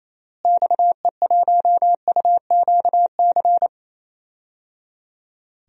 Morse code, heard as a keyed tone, about 28 wpm, 700 Hz. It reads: XE1UQC
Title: DXCC-100 Call Signs - 6 Characters - Repeated 3x - 28wpm